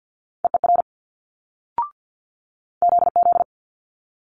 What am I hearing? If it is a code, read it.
Morse code, heard as a keyed tone, about 50 wpm, 700 Hz. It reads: EER ZZ
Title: Multi-letter Phonemes Quick Fire 50wpm